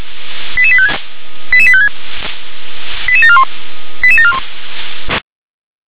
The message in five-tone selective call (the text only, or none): 88856 88856 88741 88741